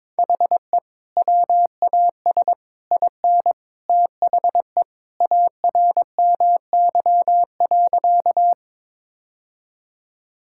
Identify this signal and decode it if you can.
Morse code, heard as a keyed tone, about 22 wpm, 700 Hz. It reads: HE WAS IN THE ARMY.
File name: practice / 22wpm / Sentences with 3-Word Phrases - 22wpm